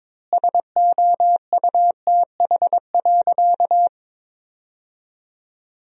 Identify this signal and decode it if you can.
Morse code, heard as a keyed tone, about 22 wpm, 700 Hz. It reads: SOUTH.
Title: 4x Word-Spacing - Sentences from Top 400 Words - 22wpm